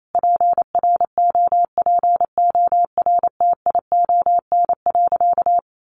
Morse code, heard as a keyed tone, about 28 wpm, 700 Hz. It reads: PROPORTION.
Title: A Sound Of Thunder 28wpm